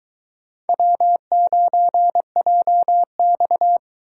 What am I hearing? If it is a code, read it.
Morse code, heard as a keyed tone, about 23 wpm, 700 Hz. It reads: W9JX